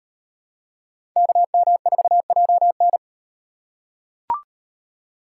Morse code, heard as a keyed tone, about 36 wpm, 700 Hz. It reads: KM4JN